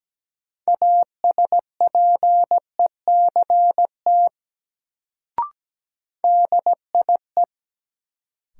Morse code, heard as a keyed tone, about 17 wpm, 700 Hz. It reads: ASPECT DIE
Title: Sentences from Top 1300 Words - New Word Drill Alt - 17wpm